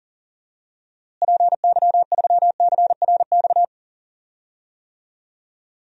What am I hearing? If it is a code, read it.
Morse code, heard as a keyed tone, about 40 wpm, 700 Hz. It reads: PY3CRX
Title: DXCC-100 Call Signs - 6 Characters - Repeated 3x - 40wpm